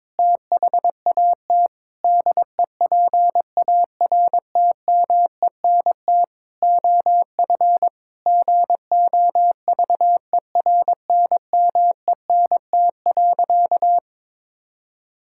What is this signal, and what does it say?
Morse code, heard as a keyed tone, about 22 wpm, 700 Hz. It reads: THAT DEPARTMENT OF GOVERNMENT.
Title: Sentences from Top 1000 Words - Review Part 5 - 22wpm